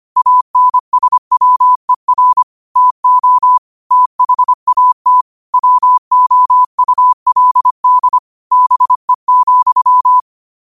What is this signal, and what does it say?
Morse code, heard as a keyed tone, about 25 wpm, 1000 Hz. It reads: ANSWER TO THAT WOULD BE,
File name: qotdfiles / 25wpm